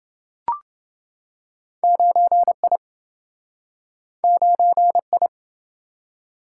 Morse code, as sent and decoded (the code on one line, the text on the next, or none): ----. .. / ----. ..
9I 9I